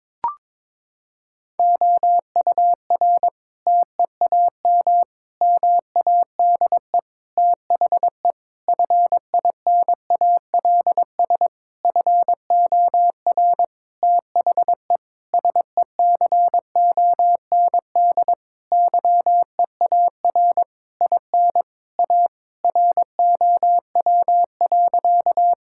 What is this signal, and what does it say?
Morse code, heard as a keyed tone, about 22 wpm, 700 Hz. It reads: OUR TEAM MADE THE FINALS FOR THE SECOND YEAR IN A ROW.
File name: Sentences with 7-Word Phrases - 22wpm